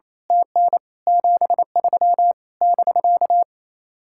Morse code, heard as a keyed tone, about 27 wpm, 700 Hz. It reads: TN 73 <BK>
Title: Speed-Racing Common QSO Phrases Encore - 28wpm